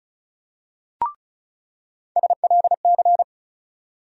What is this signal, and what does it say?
Morse code, heard as a keyed tone, about 34 wpm, 700 Hz. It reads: SLC
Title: USA Call Sign Postfixes - 3 Letters 35wpm